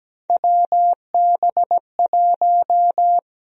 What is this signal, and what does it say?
Morse code, heard as a keyed tone, about 17 wpm, 700 Hz. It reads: WB1
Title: USA Call Sign Prefixes - 17wpm